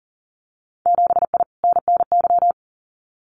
Morse code, heard as a keyed tone, about 40 wpm, 700 Hz. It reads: 7I NNY
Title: Winter Field Day Exchanges Encore 40wpm